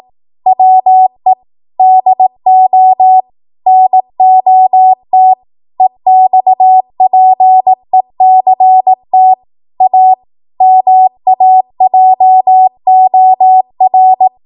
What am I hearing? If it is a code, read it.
Morse code, heard as a keyed tone, about 18 wpm, 750 Hz. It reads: WE DO NOT EXPECT A MAJOR
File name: Morse / 18WPM